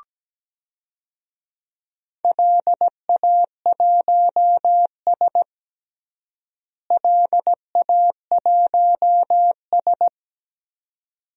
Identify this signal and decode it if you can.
Morse code, heard as a keyed tone, about 17 wpm, 700 Hz. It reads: LA1S LA1S
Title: DXCC-100 Call Signs - 4 Characters - Repeated 2x - 17wpm